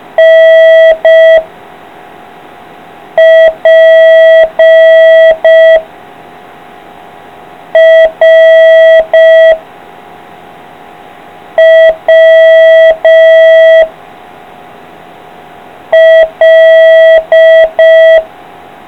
Morse code, heard as a keyed tone, about 5 wpm, 650 Hz. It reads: N P R W L